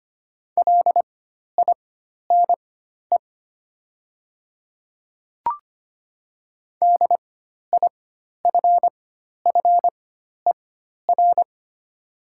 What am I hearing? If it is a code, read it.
Morse code, heard as a keyed tone, about 25 wpm, spaced as for 12 wpm, 700 Hz. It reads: LINE DIFFER
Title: 4x Character Spacing - Top 300 Words - 25wpm